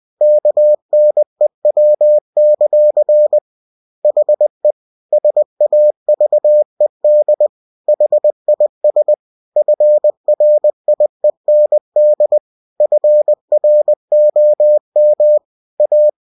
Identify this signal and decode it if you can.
Morse code, heard as a keyed tone, about 20 wpm, 600 Hz. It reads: KNEW; HE SAVED HIS FRIEND FROM A
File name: cw-books / winnie-the-pooh / winnie-20wpm-page-0103